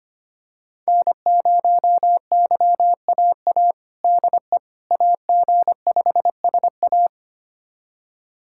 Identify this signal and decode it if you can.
Morse code, heard as a keyed tone, about 25 wpm, 700 Hz. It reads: N0YAA DE AG5SA
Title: USA Callsign DE Callsign 25wpm